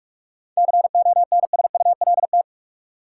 Morse code, heard as a keyed tone, about 43 wpm, 700 Hz. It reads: CONSULT
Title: Seven Letter Words - 45wpm